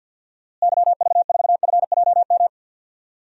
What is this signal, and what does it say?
Morse code, heard as a keyed tone, about 46 wpm, 700 Hz. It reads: YV4FJM
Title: DXCC-100 Call Signs - 6 Characters - Repeated 2x - 50wpm